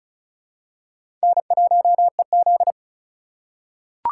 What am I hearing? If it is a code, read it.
Morse code, heard as a keyed tone, about 36 wpm, 700 Hz. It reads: N1EZ
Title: USA Call Signs - 2x Character Spacing - 35wpm